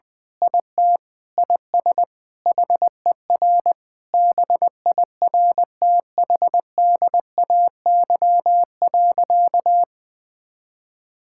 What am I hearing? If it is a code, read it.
Morse code, heard as a keyed tone, about 20 wpm, 700 Hz. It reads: IT IS HER BIRTHDAY.
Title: Sentences with 6-Word Phrases - 20wpm